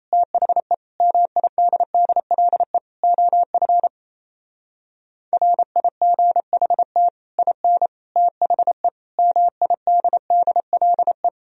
Morse code, heard as a keyed tone, about 35 wpm, 700 Hz. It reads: THE MIDDLE OF RIGHT IN THE MIDDLE